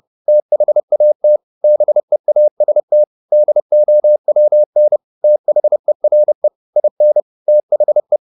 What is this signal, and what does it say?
Morse code, heard as a keyed tone, about 30 wpm, 600 Hz. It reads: THAT BEAST DOWN THERE IN THE
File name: the old man and the sea / the old man and the sea 1 30wpm